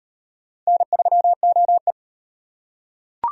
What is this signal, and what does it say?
Morse code, heard as a keyed tone, about 37 wpm, 700 Hz. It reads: N3OE